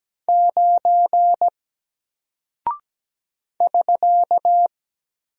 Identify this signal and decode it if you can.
Morse code, heard as a keyed tone, about 17 wpm, 700 Hz. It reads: 9 <SK>
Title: Single Character (Letter:Number:Punctuation:Prosign) 17wpm